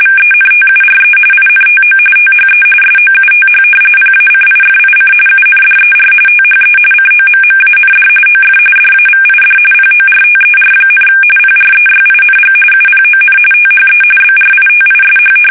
RTTY, about 45.45 baud. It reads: VENTURE INTO THE SOUTH PACIFIC UNDER THE COMMAND OF DAVID PORTER, PREYING UPON THE BRITISH
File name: fsk850 / usnavy2